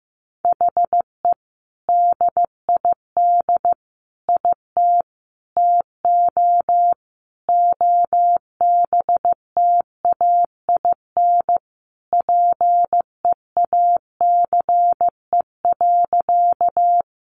Morse code, heard as a keyed tone, about 15 wpm, 700 Hz. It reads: HE DID IT TO OBTAIN PEACE.
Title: Sentences from Top 1500 Words 15wpm